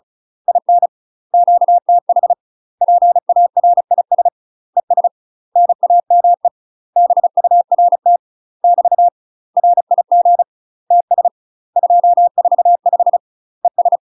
Morse code, heard as a keyed tone, about 36 wpm, 700 Hz. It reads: IN QTH PARIS ES NAME BURT <BT> RIG TS 245 ES
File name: UK MASTER QSOs TRAINING / 35WPM / UK MASTER QSOs 0017